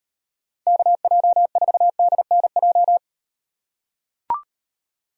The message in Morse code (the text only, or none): KJ4DNJ